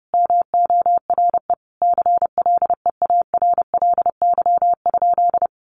Morse code, heard as a keyed tone, about 30 wpm, 700 Hz. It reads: MORE CLEARLY?
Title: Sentences from Top 1300 Words 30wpm